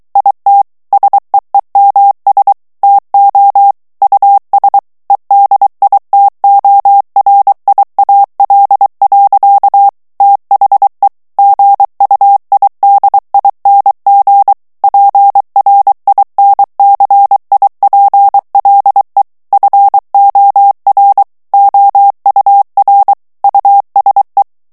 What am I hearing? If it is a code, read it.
Morse code, heard as a keyed tone, about 23 wpm, 800 Hz. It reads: IT SEEMS TO US EDITORIAL. THE GUIDING PRINCIPLE FOR OUR USE
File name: ANGLAIS texte clair 3 vitesse 22 mots minute